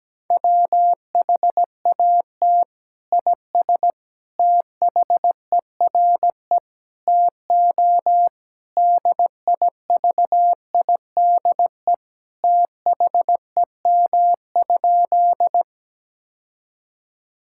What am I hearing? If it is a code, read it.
Morse code, heard as a keyed tone, about 17 wpm, 700 Hz. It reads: WHAT IS THERE TO DIVIDE THEM?
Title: Sentences from Top 1600 Words 17wpm